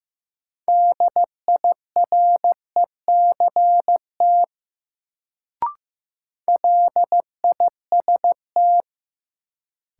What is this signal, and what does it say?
Morse code, heard as a keyed tone, about 15 wpm, 700 Hz. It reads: DIRECT LIST
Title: Sentences from Top 400 Words - New Word Drill - 15wpm